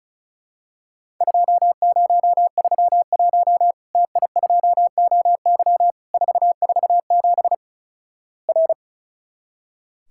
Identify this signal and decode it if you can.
Morse code, heard as a keyed tone, about 34 wpm, 700 Hz. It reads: 2031 TI2OY 447 R
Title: Worked All Europe DX Contest - QTC Exchange - 35wpm